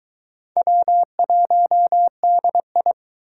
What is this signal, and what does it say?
Morse code, heard as a keyed tone, about 23 wpm, 700 Hz. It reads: W1DI